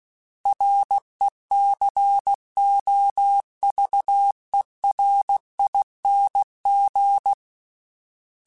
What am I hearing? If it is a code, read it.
Morse code, heard as a keyed tone, about 16 wpm, 800 Hz. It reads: RECOVERING
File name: Special INGs / Special INGs-mp3s / Sngl ing @15